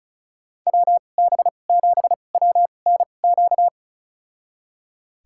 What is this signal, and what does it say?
Morse code, heard as a keyed tone, about 36 wpm, spaced as for 25 wpm, 700 Hz. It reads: WB7WNQ
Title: USA Call Signs - 2x Character Spacing - 35wpm